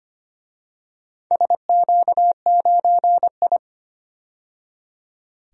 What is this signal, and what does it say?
Morse code, heard as a keyed tone, about 25 wpm, 700 Hz. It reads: SQ9I
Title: DXCC-100 Call Signs - 4 Characters - 25wpm